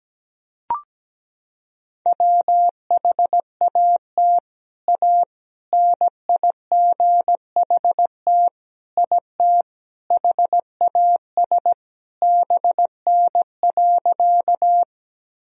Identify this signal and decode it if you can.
Morse code, heard as a keyed tone, about 17 wpm, 700 Hz. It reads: WHAT A NIGHT IT HAS BN.
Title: Sentences from Top 300 Words and Abbreviations 17wpm